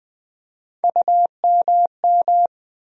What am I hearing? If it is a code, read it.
Morse code, heard as a keyed tone, about 20 wpm, 700 Hz. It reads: UMM
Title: USA Call Sign Postfixes - 3 Letters 20wpm